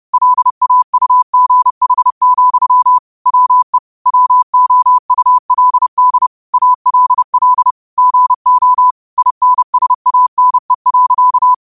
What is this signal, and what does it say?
Morse code, heard as a keyed tone, about 30 wpm, 1000 Hz. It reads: LAUGH, WE WOULD ALL GO INSANE.
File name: qotdfiles / 30wpm